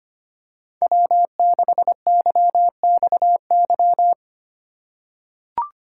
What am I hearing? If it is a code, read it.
Morse code, heard as a keyed tone, about 25 wpm, 700 Hz. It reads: W6YXY
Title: USA Call Signs - 5 Characters - 3x Character Spacing - 25wpm